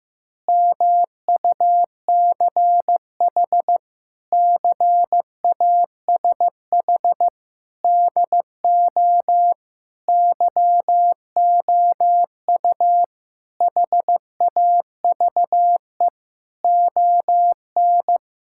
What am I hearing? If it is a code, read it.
Morse code, heard as a keyed tone, about 14 wpm, 700 Hz. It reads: MUCH CASH DO YOU HAVE ON
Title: Sentences from Top 1000 Words - Review Part 5 - 15wpm